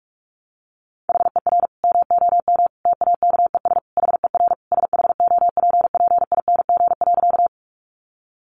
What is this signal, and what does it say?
Morse code, heard as a keyed tone, about 44 wpm, 700 Hz. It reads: HER MOM TAKES HER SHOPPING.
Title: Additional Easy Phrases 45wpm